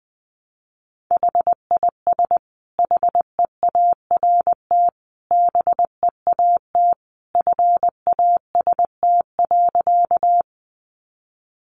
Morse code, heard as a keyed tone, about 20 wpm, 700 Hz. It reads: HIS HEART BEAT FAST.
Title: Sentences from Top 1300 Words 20wpm